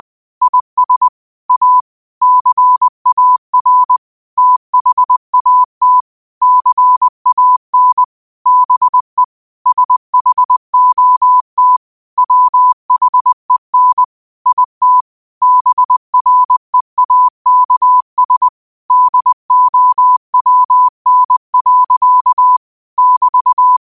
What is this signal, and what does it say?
Morse code, heard as a keyed tone, about 20 wpm, 1000 Hz. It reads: IS A CAR THAT CAN BE SHOT WHEN IT BREAKS DOWN. <BT>
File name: qotdfiles / 20wpm